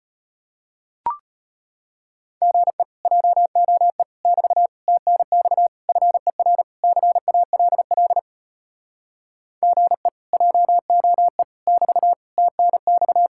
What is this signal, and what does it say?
Morse code, heard as a keyed tone, about 33 wpm, 700 Hz. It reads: GE JOE <BT> TNX FER CALL GE JOE <BT> TNX